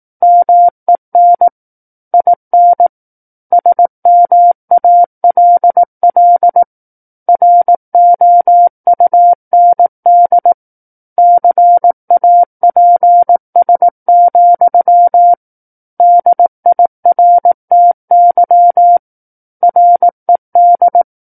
Morse code, heard as a keyed tone, about 18 wpm, 700 Hz. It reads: MEN IN SMALL ROUND CAPS, DIRTY RED
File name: 18 WPM